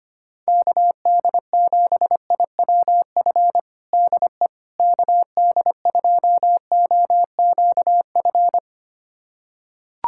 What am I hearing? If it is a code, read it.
Morse code, heard as a keyed tone, about 25 wpm, 700 Hz. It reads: KD7IWF DE KD2OQF
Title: USA Callsign DE Callsign - 6 Characters - 25wpm